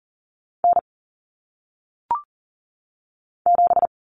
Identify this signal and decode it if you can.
Morse code, heard as a keyed tone, about 40 wpm, 700 Hz. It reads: N 7
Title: Common Cut Numbers - 1-Digit 40wpm